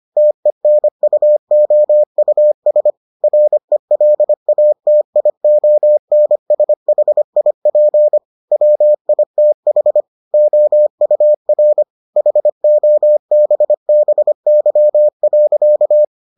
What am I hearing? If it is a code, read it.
Morse code, heard as a keyed tone, about 25 wpm, 600 Hz. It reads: TENUOUS RELATIONSHIP WITH OUR HOBBY.